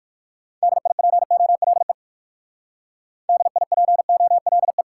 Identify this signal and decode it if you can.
Morse code, heard as a keyed tone, about 56 wpm, 700 Hz. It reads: DIPOLE DIPOLE
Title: Speed-Racing Top 90 QSO Elements 45wpm